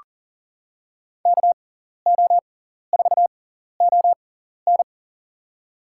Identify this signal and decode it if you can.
Morse code, heard as a keyed tone, about 40 wpm, spaced as for 14 wpm, 690 Hz. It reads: KO4ON
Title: USA Call Signs - 5 Characters - 6x Character Spacing - 40wpm